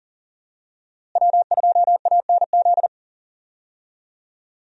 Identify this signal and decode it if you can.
Morse code, heard as a keyed tone, about 40 wpm, 700 Hz. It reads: W2ANZ
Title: USA Call Signs - 5 Characters - Repeated 3x - 40wpm